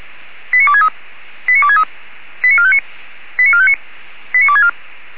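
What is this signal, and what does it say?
Five-tone selective call (ZVEI), heard as quick runs of five tones, tones of about 70 ms. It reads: 89263 89263 89459 89459 89264
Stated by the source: Received on an SDR as ZVEI five-tone call, address 89269.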